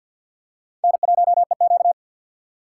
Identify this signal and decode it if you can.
Morse code, heard as a keyed tone, about 47 wpm, 700 Hz. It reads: N1EQ
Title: USA Call Signs - 4 Characters - 2x Character Spacing - 50wpm